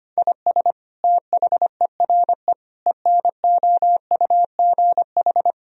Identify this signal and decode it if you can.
Morse code, heard as a keyed tone, about 25 wpm, 700 Hz. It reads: IS THERE ENOUGH
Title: Sentences from Top 1000 Words - Review Part 1 - 25wpm